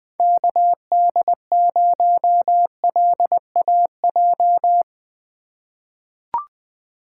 Morse code, heard as a keyed tone, about 21 wpm, 700 Hz. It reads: KD0LAJ
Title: USA Call Signs - 3x Character Spacing - 20wpm